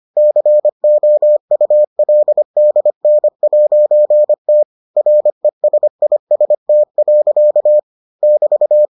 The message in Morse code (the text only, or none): COULDN'T RESIST. <BT>